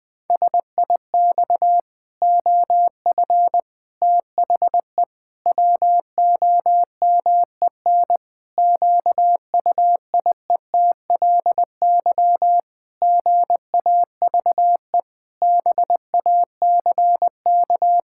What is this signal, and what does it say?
Morse code, heard as a keyed tone, about 20 wpm, 700 Hz. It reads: SIX OF THE WOMEN QUIETLY GAVE BACK
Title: Pangram Challenge v2 - 20wpm